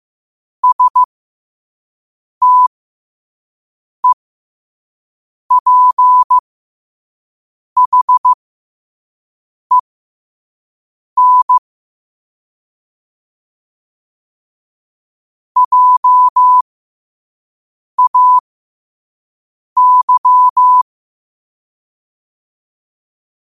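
Morse code, heard as a keyed tone, about 15 wpm, spaced as for 5 wpm, 1000 Hz. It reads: STEPHEN JAY
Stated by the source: Morse sent at 5 wpm.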